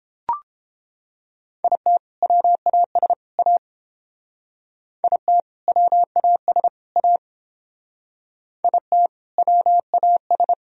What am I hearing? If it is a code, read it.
Morse code, heard as a keyed tone, about 29 wpm, 700 Hz. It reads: IT WAS A IT WAS A IT WAS